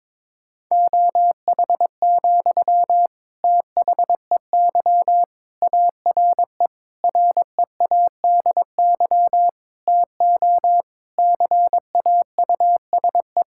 Morse code, heard as a keyed tone, about 22 wpm, 700 Hz. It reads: OH, THEY ARE READY TO CAUSE
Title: Sentences from Top 1600 Words 22wpm